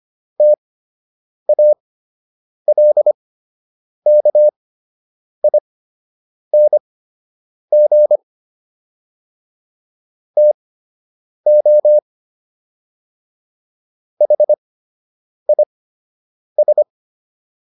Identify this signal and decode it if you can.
Morse code, heard as a keyed tone, about 25 wpm, spaced as for 8 wpm, 600 Hz. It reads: TALKING TO HIS